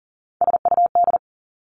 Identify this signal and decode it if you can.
Morse code, heard as a keyed tone, about 42 wpm, 690 Hz. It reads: SUD
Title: Three Challenging Letter-Number Combinations 40wpm